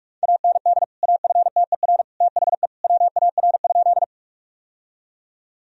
Morse code, heard as a keyed tone, about 45 wpm, 700 Hz. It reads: AND AFTER THE WAR?
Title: Sentences from Top 1000 Words - Review Part 2 - 45wpm